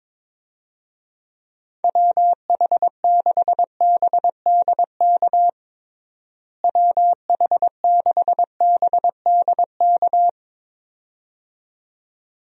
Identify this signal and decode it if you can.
Morse code, heard as a keyed tone, about 22 wpm, 700 Hz. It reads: WH6BDK WH6BDK
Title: USA Call Signs - 6 Characters - Repeated 2x - 22wpm